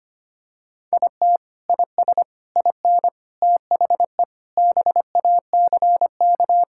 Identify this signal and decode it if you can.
Morse code, heard as a keyed tone, about 25 wpm, 700 Hz. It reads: IT IS IN THE BACK
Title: Sentences from Top 1000 Words - Review Part 5 - 25wpm